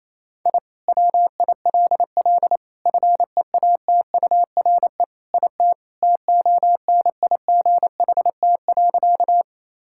Morse code, heard as a keyed tone, about 28 wpm, 700 Hz. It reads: I WILL FEATURE IT TONIGHT.
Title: Sentences from Top 1000 Words - Review Part 5 - 28wpm